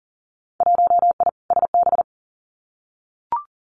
Morse code, heard as a keyed tone, about 40 wpm, 700 Hz. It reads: 1I SB
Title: Winter Field Day Exchanges Encore 40wpm